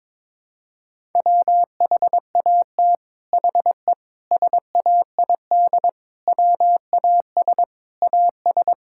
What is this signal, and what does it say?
Morse code, heard as a keyed tone, about 22 wpm, 700 Hz. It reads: WHAT HE SAID WAS AS
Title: Sentences from Top 1000 Words - Review Part 6 - 22wpm